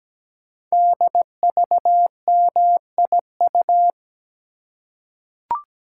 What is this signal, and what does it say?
Morse code, heard as a keyed tone, about 17 wpm, 700 Hz. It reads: DVMIU
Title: Code Groups - 5 Letters - 17wpm